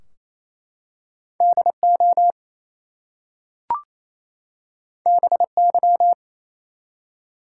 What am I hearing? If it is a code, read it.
Morse code, heard as a keyed tone, about 29 wpm, 700 Hz. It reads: DO BY
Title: Two Letter Words - 28wpm